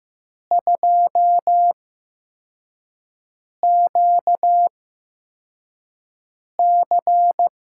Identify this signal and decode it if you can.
Morse code, heard as a keyed tone, about 15 wpm, spaced as for 4 wpm, 700 Hz. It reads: 2QC
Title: USA Call Signs - 6 Characters - 8x Character Spacing - 15wpm